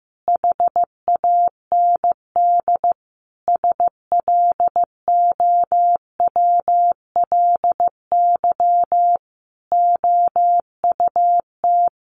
Morse code, heard as a keyed tone, about 15 wpm, 700 Hz. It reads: HAND SLOWLY OUT
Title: A Sound Of Thunder 15wpm